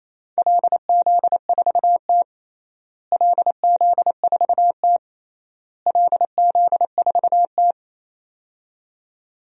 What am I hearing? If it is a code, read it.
Morse code, heard as a keyed tone, about 28 wpm, 700 Hz. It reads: LZ4T LZ4T LZ4T
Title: DXCC-100 Call Signs - 4 Characters - Repeated 3x - 28wpm